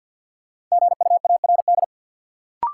Morse code, heard as a keyed tone, about 46 wpm, 700 Hz. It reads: GUARD
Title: Sentences from Top 700 Words - New Word Drill - 50wpm